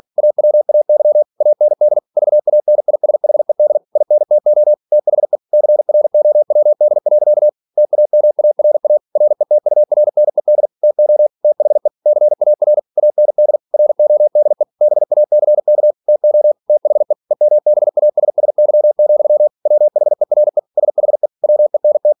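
Morse code, heard as a keyed tone, about 46 wpm, 600 Hz. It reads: AWAY AND VANISHED INTO THE CROWD. TAMARA RETURNED TO THE CAR AND RODE BACK TO THE EMBASSY, WHERE SHE WENT
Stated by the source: Practice text from an English novel sent at 47 WPM.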